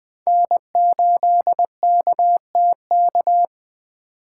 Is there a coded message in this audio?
Yes: Morse code from 0.3 to 3.5 s, heard as a keyed tone, about 19 wpm, 700 Hz. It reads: N8KTK